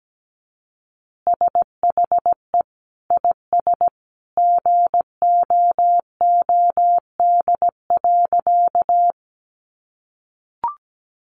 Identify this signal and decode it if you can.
Morse code, heard as a keyed tone, about 17 wpm, 700 Hz. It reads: SHE IS GOOD.
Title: Easy Phrases 17wpm